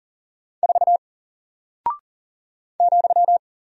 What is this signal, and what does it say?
Morse code, heard as a keyed tone, about 40 wpm, 700 Hz. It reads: 4 ,
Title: Single Character (Letter:Number:Punctuation:Prosign) 40wpm